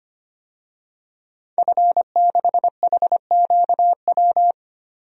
Morse code, heard as a keyed tone, about 25 wpm, 700 Hz. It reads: F6HQW
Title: DXCC-100 Call Signs - 5 Characters - Repeated 2x - 25wpm